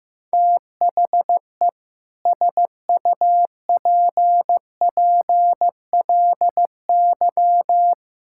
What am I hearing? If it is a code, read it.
Morse code, heard as a keyed tone, about 15 wpm, 700 Hz. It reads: THE SUPPLY
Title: Sentences from Top 1000 Words - Review Part 6 - 15wpm